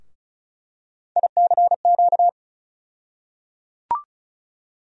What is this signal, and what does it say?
Morse code, heard as a keyed tone, about 36 wpm, 700 Hz. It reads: ICQ
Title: USA Call Sign Postfixes - 3 Letters 35wpm